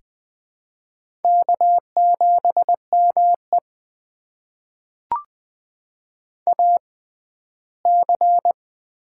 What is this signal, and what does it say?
Morse code, heard as a keyed tone, about 20 wpm, spaced as for 14 wpm, 700 Hz. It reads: K7ME AC
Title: USA Call Signs - 4 Characters - 6x Character Spacing - 20wpm